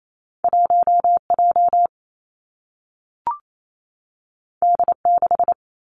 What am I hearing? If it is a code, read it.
Morse code, heard as a keyed tone, about 28 wpm, 700 Hz. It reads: 1 J D 6
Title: Two Challenging Letter-Number Combinations 28wpm